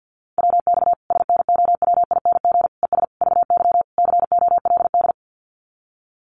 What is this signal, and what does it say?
Morse code, heard as a keyed tone, about 50 wpm, 700 Hz. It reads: WX SNOWING ES VY COLD
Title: Common QSO Phrases 50wpm